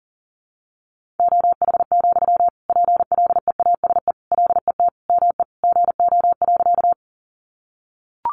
Morse code, heard as a keyed tone, about 40 wpm, 700 Hz. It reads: OH, PLEASE LET ME GO.
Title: Sentences from Top 1500 Words 40wpm